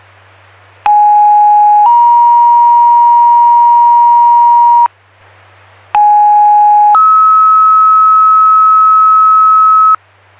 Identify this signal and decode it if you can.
Two-tone page, heard as a one-second tone, then a three-second tone, about 830 then 950 Hz, about 830 then 1290 Hz. It is heard from 0.9 to 10.0 s.